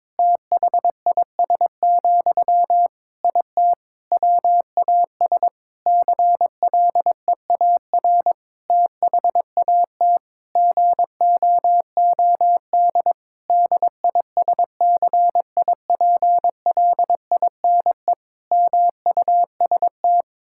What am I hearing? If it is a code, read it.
Morse code, heard as a keyed tone, about 22 wpm, 700 Hz. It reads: THIS, IT WAS CLEAR THAT GOOD DISCIPLINE MUST
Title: Sentences from Top 1000 Words - Review Part 6 - 22wpm